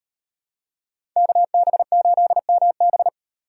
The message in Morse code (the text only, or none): KB8MB